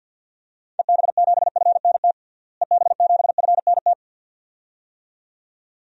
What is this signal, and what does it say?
Morse code, heard as a keyed tone, about 52 wpm, 700 Hz. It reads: EB7FNT EB7FNT
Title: DXCC-100 Call Signs - 6 Characters - Repeated 2x - 50wpm